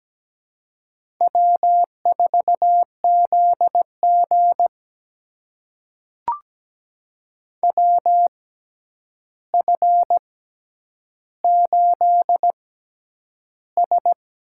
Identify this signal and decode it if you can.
Morse code, heard as a keyed tone, about 17 wpm, spaced as for 9 wpm, 700 Hz. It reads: W4ZG WF8S
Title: USA Call Signs - 6x Character Spacing - 17wpm